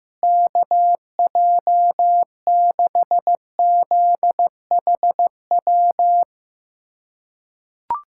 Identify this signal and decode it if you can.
Morse code, heard as a keyed tone, about 16 wpm, 700 Hz. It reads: KJ6ZHW
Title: USA Callsign DE Callsign - 6 Characters - 15wpm